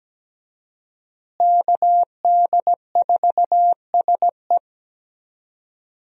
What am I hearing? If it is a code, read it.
Morse code, heard as a keyed tone, about 17 wpm, 700 Hz. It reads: KD4SE
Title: USA Call Signs - 5 Characters - Repeated 2x - 17wpm